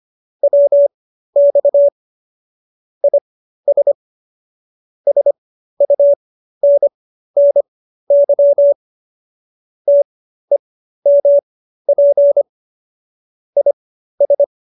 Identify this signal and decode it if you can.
Morse code, heard as a keyed tone, about 25 wpm, spaced as for 13 wpm, 570 Hz. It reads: WX IS SUNNY TEMP IS
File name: qso208b 13